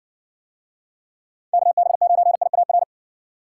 Morse code, heard as a keyed tone, about 59 wpm, 700 Hz. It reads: KB9IAR